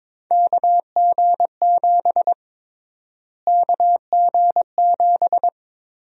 Morse code, heard as a keyed tone, about 22 wpm, 700 Hz. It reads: KG7 KG7
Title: USA Call Sign Prefixes - Repeated 2x - 22wpm